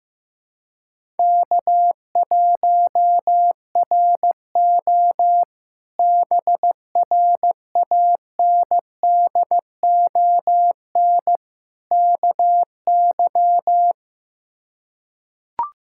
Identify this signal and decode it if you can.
Morse code, heard as a keyed tone, about 15 wpm, 700 Hz. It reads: K1RO BRANDON KY
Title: K1USN SST QSO Practice - 15wpm